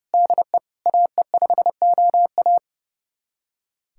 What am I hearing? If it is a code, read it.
Morse code, heard as a keyed tone, about 32 wpm, 700 Hz. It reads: DE AE5OA
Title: USA Callsign DE Callsign 30wpm